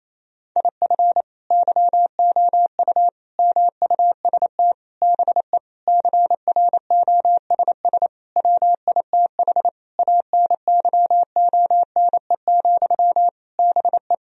Morse code, heard as a keyed tone, about 28 wpm, 700 Hz. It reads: IF YOU MUST BE CROSS WITH ANYONE, BE
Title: Sentences from Top 1000 Words - Review Part 1 - 28wpm